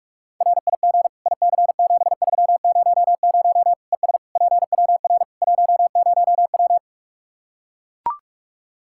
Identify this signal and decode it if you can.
Morse code, heard as a keyed tone, about 42 wpm, 700 Hz. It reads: RIG IC7300 ES PWR 10W
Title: K1USN SST QSO Practice 45wpm